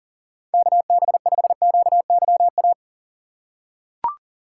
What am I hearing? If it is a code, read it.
Morse code, heard as a keyed tone, about 40 wpm, 700 Hz. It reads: KB5QYA
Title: USA Call Signs - 2x Character Spacing - 40wpm